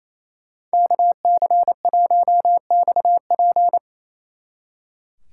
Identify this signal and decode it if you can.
Morse code, heard as a keyed tone, about 34 wpm, 700 Hz. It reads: KC1XP
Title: Speed Racing - USA 2x2 Call Signs 28wpm